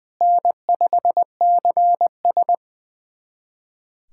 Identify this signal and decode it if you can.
Morse code, heard as a keyed tone, about 20 wpm, 700 Hz. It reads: N5CS
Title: USA Callsign DE Callsign - 4 Characters - 20wpm